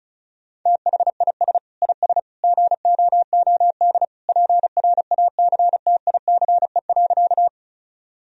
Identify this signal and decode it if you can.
Morse code, heard as a keyed tone, about 36 wpm, 700 Hz. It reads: THIS IS GOOD PRACTICE.
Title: Sentences from Top 1000 Words - Review Part 4 - 35wpm